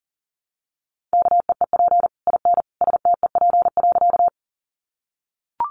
Morse code, heard as a keyed tone, about 40 wpm, 700 Hz. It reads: KEEP IN STEP.
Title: Sentences from Top 400 Words 40wpm